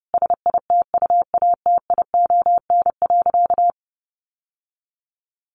Morse code, heard as a keyed tone, about 28 wpm, 700 Hz. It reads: SITUATION.
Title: Sentences from Top 1200 Words 30wpm